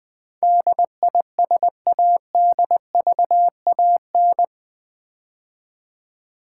Morse code, heard as a keyed tone, about 20 wpm, 700 Hz. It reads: DISADVAN
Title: Build A Word - 4 Letters At A Time - 12-Letter Words - 20wpm